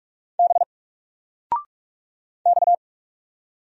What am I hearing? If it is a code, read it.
Morse code, heard as a keyed tone, about 45 wpm, 700 Hz. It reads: B X
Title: Single Letter-Number 45wpm